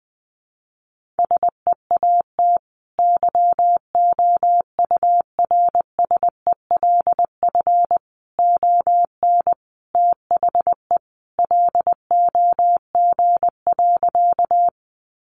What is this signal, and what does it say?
Morse code, heard as a keyed tone, about 20 wpm, 700 Hz. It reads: SEAT YOURSELF ON THE LOG.
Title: Sentences from Top 1500 Words 20wpm